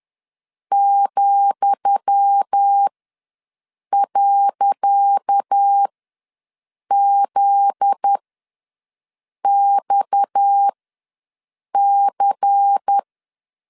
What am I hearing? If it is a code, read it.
Morse code, heard as a keyed tone, about 10 wpm, spaced as for 6 wpm, 800 Hz. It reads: ,.ZXC
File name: LEZIONI / 10 SIGNS / 10D COMA ws 10wpm 4314